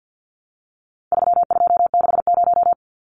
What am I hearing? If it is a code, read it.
Morse code, heard as a keyed tone, about 50 wpm, 700 Hz. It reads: 3260
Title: Numbers - 4-Digit 50wpm